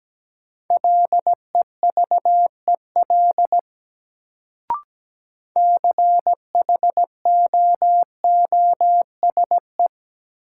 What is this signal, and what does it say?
Morse code, heard as a keyed tone, about 17 wpm, 700 Hz. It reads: LEVEL CHOOSE
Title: Sentences from Top 700 Words - New Word Drill - 17wpm